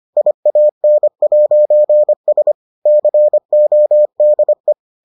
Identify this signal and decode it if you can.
Morse code, heard as a keyed tone, about 25 wpm, 600 Hz. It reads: IAN'S CODE